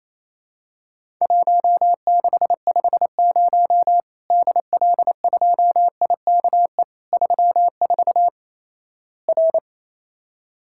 Morse code, heard as a keyed tone, about 27 wpm, 700 Hz. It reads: 1650 DL2IKE 34 R
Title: Worked All Europe DX Contest - QTC Exchange - 28wpm